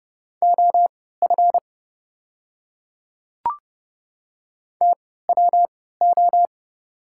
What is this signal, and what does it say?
Morse code, heard as a keyed tone, about 30 wpm, spaced as for 17 wpm, 690 Hz. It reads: OF TWO
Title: 3x Character Spacing - Top 100 Words - 30wpm